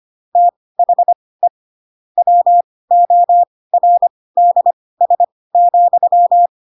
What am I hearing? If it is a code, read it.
Morse code, heard as a keyed tone, about 25 wpm, spaced as for 18 wpm, 700 Hz. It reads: THE WORDS,
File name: Pinocchio / Pinocchio ch2 12 25 ik1hge